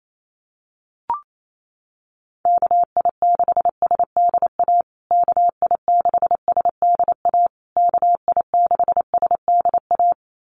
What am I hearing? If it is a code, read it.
Morse code, heard as a keyed tone, about 28 wpm, 700 Hz. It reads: KI6SDA KI6SDA KI6SDA
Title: USA Call Signs Repeated 3x 28wpm